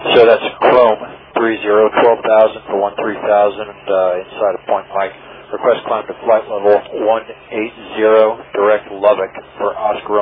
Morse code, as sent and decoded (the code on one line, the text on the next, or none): none